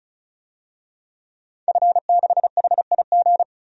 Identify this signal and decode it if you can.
Morse code, heard as a keyed tone, about 35 wpm, 700 Hz. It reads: F6HIG
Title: DXCC-100 Call Signs - 5 Characters - 35wpm